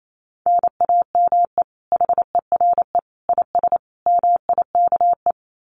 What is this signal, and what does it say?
Morse code, heard as a keyed tone, about 28 wpm, 700 Hz. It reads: NAME HERE IS MIKE
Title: Common QSO Phrases 28wpm